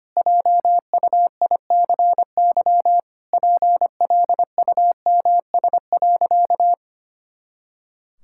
Morse code, heard as a keyed tone, about 25 wpm, 700 Hz. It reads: JUICY PLUMS.
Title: Pangram Challenge v2 - 25wpm